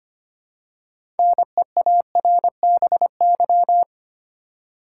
Morse code, heard as a keyed tone, about 25 wpm, 700 Hz. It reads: NEARBY
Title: Six Letter Words - 25wpm